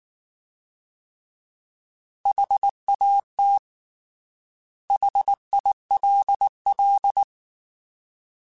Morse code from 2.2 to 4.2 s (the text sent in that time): HAT